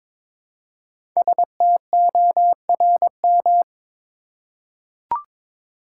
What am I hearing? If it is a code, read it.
Morse code, heard as a keyed tone, about 22 wpm, 700 Hz. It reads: STORM
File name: 5 Letter Words - First 3 Letters Given - 22wpm